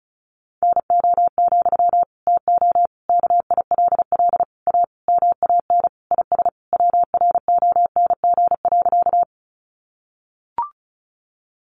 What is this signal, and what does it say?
Morse code, heard as a keyed tone, about 34 wpm, 700 Hz. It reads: NO, TO KILL A MAN IS WRONG.
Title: Sentences from Top 1400 Words 35wpm